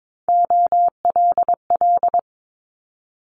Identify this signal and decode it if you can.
Morse code, heard as a keyed tone, about 22 wpm, 700 Hz. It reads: OLL
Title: Multi-letter Phonemes Quick Fire 22wpm